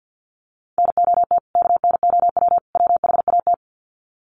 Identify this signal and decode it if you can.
Morse code, heard as a keyed tone, about 50 wpm, 700 Hz. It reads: NOT KNOW WHAT
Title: Sets of 3 Words 50wpm